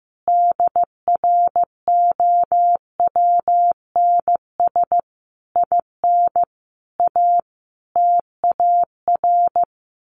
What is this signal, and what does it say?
Morse code, heard as a keyed tone, about 15 wpm, 700 Hz. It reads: DROWNS IN A TAR
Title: A Sound Of Thunder 15wpm